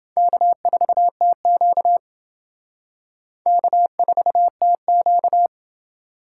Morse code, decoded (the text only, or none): K4TQ K4TQ